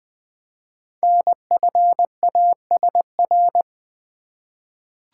Morse code, heard as a keyed tone, about 20 wpm, 700 Hz. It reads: NFASR
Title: Code Groups - 5 Letters - 20wpm